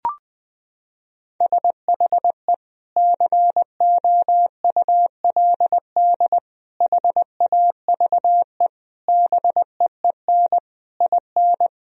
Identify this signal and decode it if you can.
Morse code, heard as a keyed tone, about 20 wpm, 700 Hz. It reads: SHE COULD HAVE BEEN IN
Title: Sentences from Top 1000 Words - Review Part 1 - 20wpm